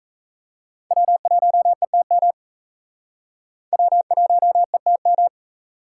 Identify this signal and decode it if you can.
Morse code, heard as a keyed tone, about 35 wpm, 700 Hz. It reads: W1ETM W1ETM
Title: Speed Racing - USA 1x3 Call Signs 28wpm